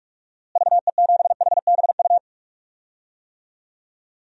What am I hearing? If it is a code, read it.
Morse code, heard as a keyed tone, about 42 wpm, 700 Hz. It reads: VE7HBU
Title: DXCC-100 Call Signs - 6 Characters - Repeated 2x - 45wpm